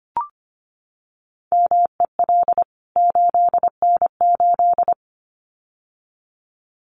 Morse code, heard as a keyed tone, about 25 wpm, 700 Hz. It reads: MEL 8N8
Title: CWT Exchange 25wpm